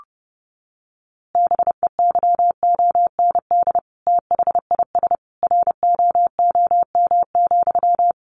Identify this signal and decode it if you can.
Morse code, heard as a keyed tone, about 30 wpm, 700 Hz. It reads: BEYOND THIS ROOM,
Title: A Sound Of Thunder 30wpm